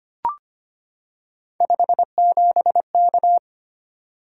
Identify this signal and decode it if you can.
Morse code, heard as a keyed tone, about 25 wpm, 700 Hz. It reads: 57K
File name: Three Letter-Number Combinations - 25wpm